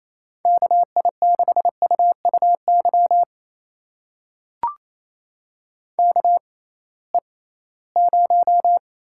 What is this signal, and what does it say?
Morse code, heard as a keyed tone, about 28 wpm, spaced as for 18 wpm, 700 Hz. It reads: KI6UUY KE0
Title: USA Call Signs - 6 Characters - 6x Character Spacing - 28wpm